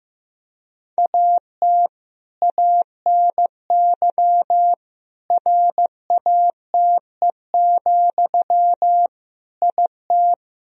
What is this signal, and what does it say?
Morse code, heard as a keyed tone, about 15 wpm, 700 Hz. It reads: AT ANY RATE, IT
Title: Sentences from Top 1000 Words - Review Part 4 - 15wpm